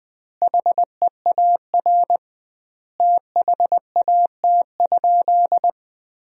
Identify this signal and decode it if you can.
Morse code, heard as a keyed tone, about 21 wpm, 700 Hz. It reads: HEAR THAT?
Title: 2x Word-Spacing - Sentences from Top 300 Words - 20wpm